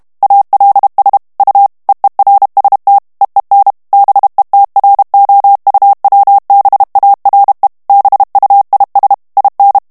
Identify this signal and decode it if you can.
Morse code, heard as a keyed tone, about 32 wpm, 800 Hz. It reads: ALS U EERST EEN BETROUWBARE BUIS IN
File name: HOLLANDAIS texte clair 2 vitesse 30 mots minute